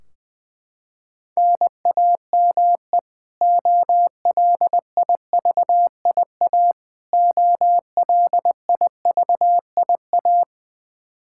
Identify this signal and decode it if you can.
Morse code, heard as a keyed tone, about 20 wpm, 700 Hz. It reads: NAME OLIVIA OLIVIA
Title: Practice QSO - Exchange 1 - Beginning - 20wpm